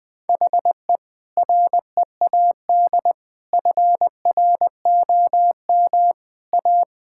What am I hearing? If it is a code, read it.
Morse code, heard as a keyed tone, about 20 wpm, 700 Hz. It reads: HE READ FROM A
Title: Sentences with 2-Word Phrases - 20wpm